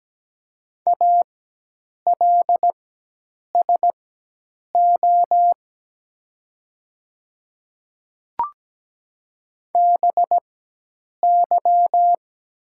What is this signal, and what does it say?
Morse code, heard as a keyed tone, about 17 wpm, spaced as for 8 wpm, 700 Hz. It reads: ALSO BY
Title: 4x Character Spacing - Top 300 Words - 17wpm